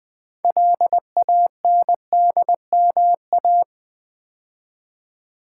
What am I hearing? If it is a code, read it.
Morse code, heard as a keyed tone, about 20 wpm, 700 Hz. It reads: LANDMA
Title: Build A Word - 2 Letters At A Time - 8-Letter Words - 20wpm